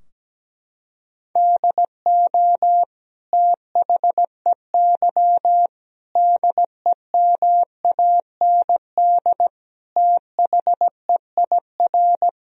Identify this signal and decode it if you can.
Morse code, heard as a keyed tone, about 17 wpm, 700 Hz. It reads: DO THEY DEMAND THEIR
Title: Sentences from Top 1000 Words - Review Part 3 - 17wpm